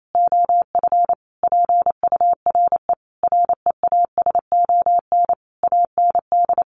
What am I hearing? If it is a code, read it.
Morse code, heard as a keyed tone, about 28 wpm, 700 Hz. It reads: OF PURE REASON AND
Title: Sentences from Top 1500 Words 28wpm